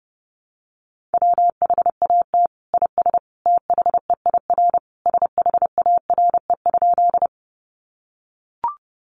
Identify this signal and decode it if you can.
Morse code, heard as a keyed tone, about 30 wpm, 700 Hz. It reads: WHAT IS THEIR SHARE?
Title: Sentences from Top 900 Words 30wpm